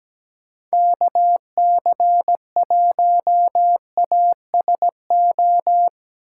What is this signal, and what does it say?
Morse code, heard as a keyed tone, about 17 wpm, 700 Hz. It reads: KC1ASO
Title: USA Call Signs - 6 Characters - 8x Character Spacing - 17wpm